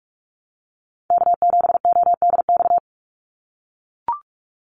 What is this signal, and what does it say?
Morse code, heard as a keyed tone, about 44 wpm, 700 Hz. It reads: K7ODX
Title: NA SOTA Activator Callsigns 45wpm